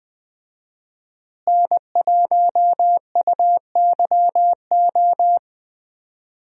Morse code, heard as a keyed tone, about 20 wpm, 700 Hz. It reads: N1UYO
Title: USA Call Signs - 5 Characters - Repeated 3x - 20wpm